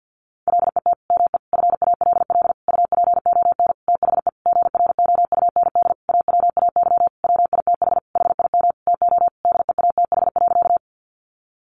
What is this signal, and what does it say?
Morse code, heard as a keyed tone, about 50 wpm, 700 Hz. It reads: LET ME FALL UPON THE GROUND AWAY WITH HIM TO DEATH.
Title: Sentences from Top 1100 Words 50wpm